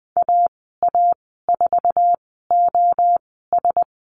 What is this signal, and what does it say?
Morse code, heard as a keyed tone, about 20 wpm, spaced as for 14 wpm, 700 Hz. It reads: AA4OS
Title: 048 Lesson 9g - Number 4 - Call Signs 20wpm